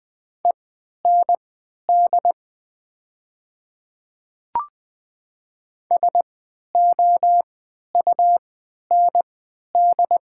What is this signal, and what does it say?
Morse code, heard as a keyed tone, about 20 wpm, spaced as for 11 wpm, 700 Hz. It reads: END SOUND
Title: 3x Character Spacing - Top 300 Words - 20wpm